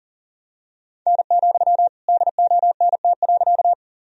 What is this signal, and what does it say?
Morse code, heard as a keyed tone, about 39 wpm, 700 Hz. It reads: N, DONT.
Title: Sentences from Top 300 Words and Abbreviations 40wpm